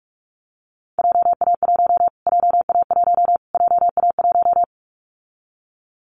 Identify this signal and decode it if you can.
Morse code, heard as a keyed tone, about 44 wpm, 700 Hz. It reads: JA1 JA1 JA1
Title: Callsign Prefixes for Top 25 DXCC - Repeated 3x - 45wpm